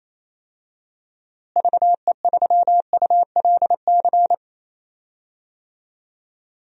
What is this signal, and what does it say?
Morse code, heard as a keyed tone, about 27 wpm, 700 Hz. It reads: VE3ULC
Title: DXCC-100 Call Signs - 6 Characters - Repeated 2x - 28wpm